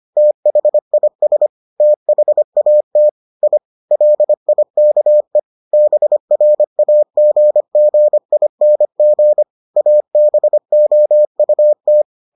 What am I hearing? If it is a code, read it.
Morse code, heard as a keyed tone, about 25 wpm, 600 Hz. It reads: THIS THAT I LIKE BRAGGING ABOUT